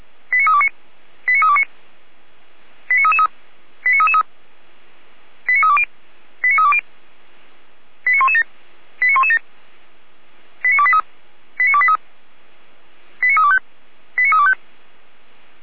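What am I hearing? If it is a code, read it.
Five-tone selective call (ZVEI), heard as quick runs of five tones, tones of about 70 ms. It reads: 89329 89329 89333 89333 89320 89320 89117 89117 89283 89283 89325 89325
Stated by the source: Received by SDR, ZVEI five-tone call 89143.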